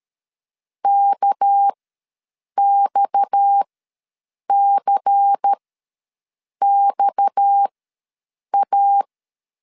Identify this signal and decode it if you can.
Morse code, heard as a keyed tone, about 13 wpm, spaced as for 7 wpm, 800 Hz. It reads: KXCXA
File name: LEZIONI / 6 PKX0 / 6 PKX ws 12wpm 3106